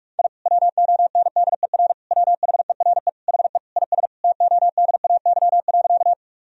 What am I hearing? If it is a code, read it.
Morse code, heard as a keyed tone, about 48 wpm, 700 Hz. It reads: I WONDER WHERE HE IS TODAY.
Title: Sentences from Top 1000 Words - Review Part 3 - 45wpm